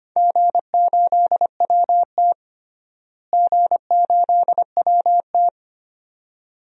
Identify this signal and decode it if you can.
Morse code, heard as a keyed tone, about 25 wpm, 700 Hz. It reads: G8WT G8WT
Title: DXCC-100 Call Signs - 4 Characters - Repeated 2x - 25wpm